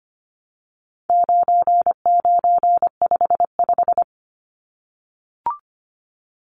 Morse code, heard as a keyed tone, about 25 wpm, 700 Hz. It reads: 9955
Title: Numbers - 4-Digit 25wpm